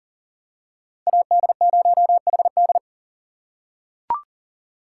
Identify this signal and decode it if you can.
Morse code, heard as a keyed tone, about 40 wpm, 700 Hz. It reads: AD0HD
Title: USA Call Signs - 5 Characters - 6x Character Spacing - 40wpm